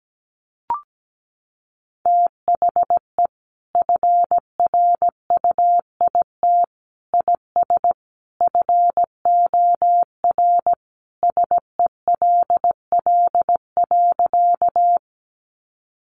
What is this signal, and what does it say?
Morse code, heard as a keyed tone, about 17 wpm, 700 Hz. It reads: THE FRUIT IS FOR SELL.
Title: Sentences from Top 1300 Words 17wpm